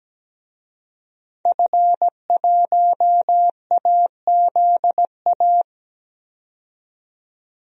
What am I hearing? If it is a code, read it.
Morse code, heard as a keyed tone, about 17 wpm, 700 Hz. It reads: F1AZA
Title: DXCC-100 Call Signs - 5 Characters - 17wpm